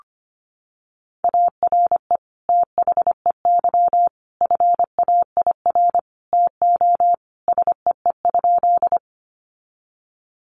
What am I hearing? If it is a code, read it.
Morse code, heard as a keyed tone, about 25 wpm, 700 Hz. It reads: ARE THEY FAIR TO SEE?
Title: Sentences from Top 1300 Words 25wpm